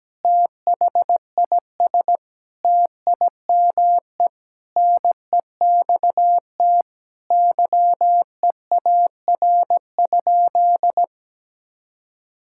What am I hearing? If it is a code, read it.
Morse code, heard as a keyed tone, about 17 wpm, 700 Hz. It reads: THIS TIME NEXT YEAR?
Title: Sentences from Top 1000 Words - Review Part 5 - 17wpm